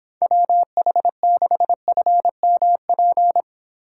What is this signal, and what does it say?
Morse code, heard as a keyed tone, about 26 wpm, 700 Hz. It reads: WH6FMP